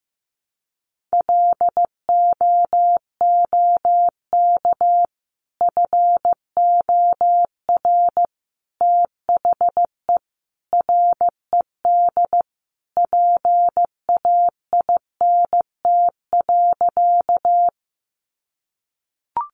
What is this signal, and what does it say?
Morse code, heard as a keyed tone, about 15 wpm, 700 Hz. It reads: LOOK FOR THE RED PAINT.
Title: A Sound Of Thunder 15wpm